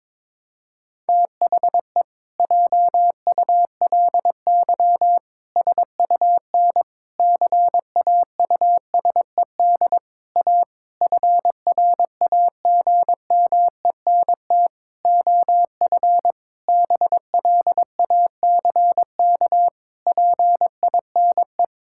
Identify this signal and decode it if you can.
Morse code, heard as a keyed tone, about 22 wpm, 700 Hz. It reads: THE JULY SUN CAUSED A FRAGMENT OF BLACK PINE
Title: Pangram Challenge v2 - 22wpm